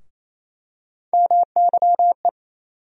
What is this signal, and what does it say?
Morse code, heard as a keyed tone, about 28 wpm, 700 Hz. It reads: MYE
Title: USA Call Sign Postfixes - 3 Letters 28wpm